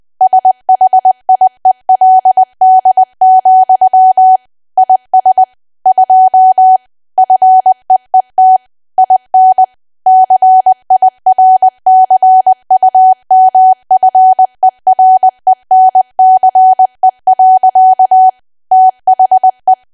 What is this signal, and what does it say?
Morse code, heard as a keyed tone, about 20 wpm, 750 Hz. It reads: SHIELD, IS 2 FEET IN CIRCUMFERENCE. THE
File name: cwa / W1AW-Level3 / 20WPM